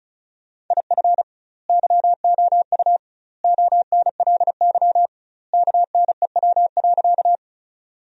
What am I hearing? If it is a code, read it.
Morse code, heard as a keyed tone, about 36 wpm, 700 Hz. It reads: IF YOU ONLY KNEW.
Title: 2x Word-Spacing - Sentences from Top 400 Words - 35wpm